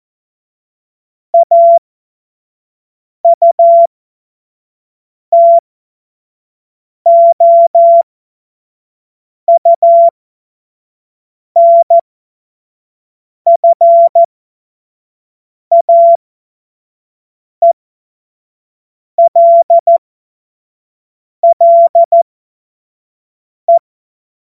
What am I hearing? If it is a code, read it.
Morse code, heard as a keyed tone, about 14 wpm, spaced as for 5 wpm, 680 Hz. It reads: AUTOUNFAELLE
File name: Morsekurs / Witze / Witz 26. April 2022 25 F70